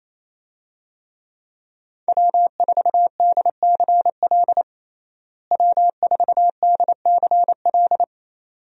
Morse code, heard as a keyed tone, about 28 wpm, 700 Hz. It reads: W4DCL W4DCL
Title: USA Call Signs - 5 Characters - Repeated 2x - 28wpm